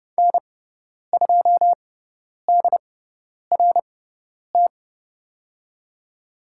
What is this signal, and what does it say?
Morse code, heard as a keyed tone, about 30 wpm, spaced as for 10 wpm, 700 Hz. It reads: N2DRT
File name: Lesson 16g - Letter c - Call Signs 30wpm